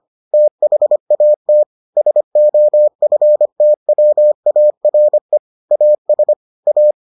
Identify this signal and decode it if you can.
Morse code, heard as a keyed tone, about 25 wpm, 600 Hz. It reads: THAT SOFTWARE AS A